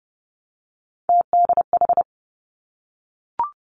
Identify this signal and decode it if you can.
Morse code, heard as a keyed tone, about 30 wpm, 700 Hz. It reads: TDH
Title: Three Challenging Letter-Number Combinations 30wpm